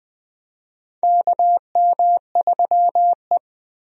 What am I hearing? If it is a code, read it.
Morse code, heard as a keyed tone, about 20 wpm, 700 Hz. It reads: KM3E
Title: USA Call Signs - 4 Characters - Repeated 3x - 20wpm